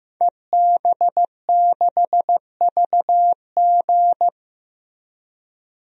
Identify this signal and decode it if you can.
Morse code, heard as a keyed tone, about 15 wpm, 700 Hz. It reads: EB6VG
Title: DXCC-100 Call Signs - 5 Characters - Repeated 2x - 15wpm